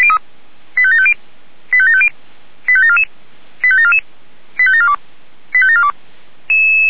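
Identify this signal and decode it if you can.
Five-tone selective call (ZVEI), heard as quick runs of five tones, tones of about 70 ms. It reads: 85750 85750 85744 85744 85742 85742
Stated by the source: ZVEI five-tone selective call 85749 received via SDR.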